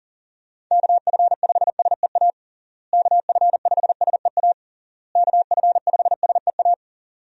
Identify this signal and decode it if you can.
Morse code, heard as a keyed tone, about 37 wpm, 700 Hz. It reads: KF5SEA KF5SEA KF5SEA
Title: USA Call Signs - 6 Characters - Repeated 3x - 40wpm